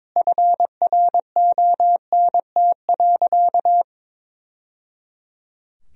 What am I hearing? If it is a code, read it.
Morse code, heard as a keyed tone, about 21 wpm, 700 Hz. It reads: FRONT.
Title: Sentences from Top 1000 Words - Review Part 2 - 22wpm